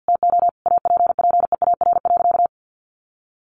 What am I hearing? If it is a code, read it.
Morse code, heard as a keyed tone, about 48 wpm, 700 Hz. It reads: TO APPEAR.
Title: Sentences from Top 1200 Words 50wpm